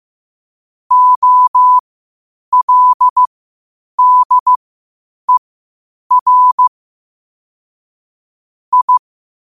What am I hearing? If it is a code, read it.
Morse code, heard as a keyed tone, about 15 wpm, spaced as for 8 wpm, 1000 Hz. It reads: OLDER I
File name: qotdfiles / 8wpm